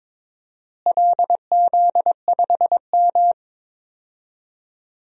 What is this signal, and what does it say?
Morse code, heard as a keyed tone, about 22 wpm, 700 Hz. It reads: LZ5M
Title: DXCC-100 Call Signs - 4 Characters - Repeated 3x - 22wpm